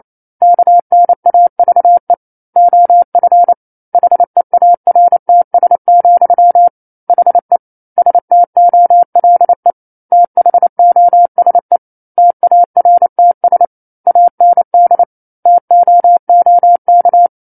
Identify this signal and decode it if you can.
Morse code, heard as a keyed tone, about 28 wpm, 700 Hz. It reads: KNAVE OF HEARTS, HE STOLE THOSE TARTS AND TOOK
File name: 09-Chapter-09 28 WPM